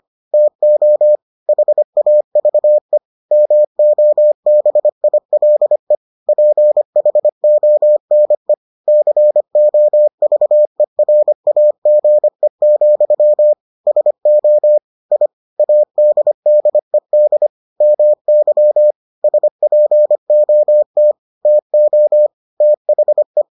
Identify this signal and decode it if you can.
Morse code, heard as a keyed tone, about 25 wpm, 600 Hz. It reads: TO HAVE MOBILE PHONE COVERAGE, SO I ADDED MY SPOT TO THE